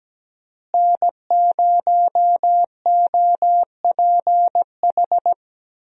Morse code, heard as a keyed tone, about 17 wpm, 700 Hz. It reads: N0OPH
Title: USA Call Signs - 5 Characters - Repeated 3x - 17wpm